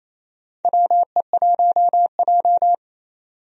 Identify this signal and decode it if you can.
Morse code, heard as a keyed tone, about 28 wpm, 700 Hz. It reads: WE1J
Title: USA Call Signs - 4 Characters - 28wpm